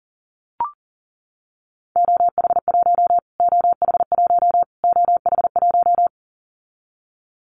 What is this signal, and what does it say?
Morse code, heard as a keyed tone, about 40 wpm, 700 Hz. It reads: OH1 OH1 OH1
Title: Callsign Prefixes for Top 25 DXCC - Repeated 3x - 40wpm